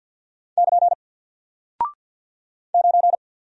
Morse code, heard as a keyed tone, about 48 wpm, 700 Hz. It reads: <KN> 9
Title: Single Character (Letter:Number:Punctuation:Prosign) Alt 50wpm